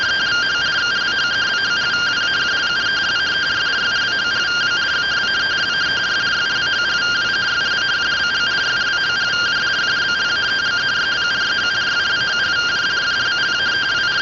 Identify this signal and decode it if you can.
Two-tone teleprinter signal, about 170 Hz shift.